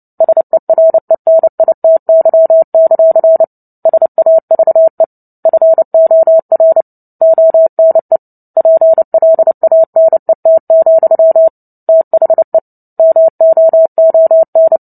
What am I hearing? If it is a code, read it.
Morse code, heard as a keyed tone, about 30 wpm, 650 Hz. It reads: SERENITY; SAVE FOR ONE PLANET, THE MOON
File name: Book Two / 03-Chapter 3 - 29 WPM